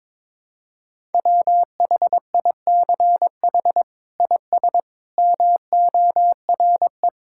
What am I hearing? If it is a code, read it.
Morse code, heard as a keyed tone, about 22 wpm, 700 Hz. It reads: WHICH IS MORE
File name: practice / 22wpm / Sets of 8 Words - 22wpm